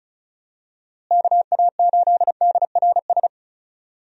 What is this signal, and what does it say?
Morse code, heard as a keyed tone, about 34 wpm, 700 Hz. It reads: KA8DRS
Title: USA Call Signs - 6 Characters - Repeated 2x - 35wpm